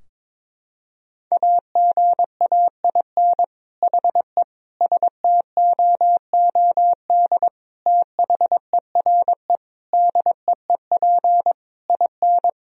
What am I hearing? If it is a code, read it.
Morse code, heard as a keyed tone, about 22 wpm, 700 Hz. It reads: AGAIN HE STOOD THERE DEEP IN
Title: Sentences from Top 1000 Words - Review Part 3 - 22wpm